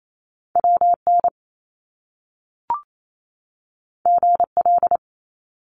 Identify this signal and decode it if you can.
Morse code, heard as a keyed tone, about 28 wpm, 700 Hz. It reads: WN GL
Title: Top Two Letter Combinations in Words 28wpm